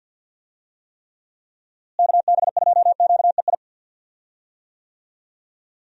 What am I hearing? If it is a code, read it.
Morse code, heard as a keyed tone, about 51 wpm, 700 Hz. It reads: KB2QEI
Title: USA Call Signs - 6 Characters - Repeated 2x - 50wpm